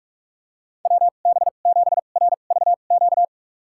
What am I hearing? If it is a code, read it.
Morse code, heard as a keyed tone, about 46 wpm, spaced as for 32 wpm, 700 Hz. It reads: WB7RVQ
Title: USA Call Signs - 6 Characters - 2x Character Spacing - 45wpm